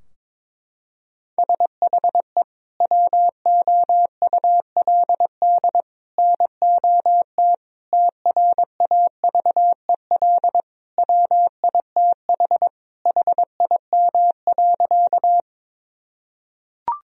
Morse code, heard as a keyed tone, about 22 wpm, 700 Hz. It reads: SHE WOULD NOT TRAVEL WITH HIM.
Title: Sentences from Top 1000 Words - Review Part 2 - 22wpm